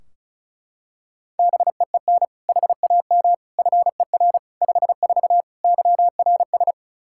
Morse code, heard as a keyed tone, about 34 wpm, 700 Hz. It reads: BEEN HAM FER 54 YRS
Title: Practice QSO - Exchange 3 - More Info - 35wpm